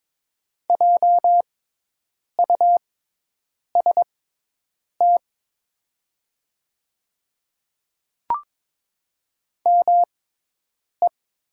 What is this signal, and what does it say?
Morse code, heard as a keyed tone, about 22 wpm, spaced as for 8 wpm, 700 Hz. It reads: JUST ME
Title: 6x Character Spacing - Top 200 Words - 22wpm